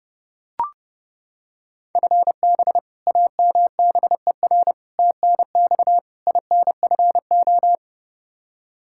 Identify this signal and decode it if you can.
Morse code, heard as a keyed tone, about 32 wpm, 700 Hz. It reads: FB AMBER TNX INFO
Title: Practice QSO - Exchange 3 - More Info - 30wpm